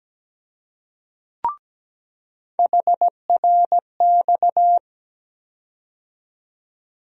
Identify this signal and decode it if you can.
Morse code, heard as a keyed tone, about 17 wpm, 700 Hz. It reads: HRX
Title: USA Call Sign Postfixes - 3 Letters 17wpm